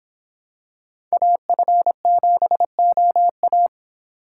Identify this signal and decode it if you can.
Morse code, heard as a keyed tone, about 26 wpm, 700 Hz. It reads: AF7OA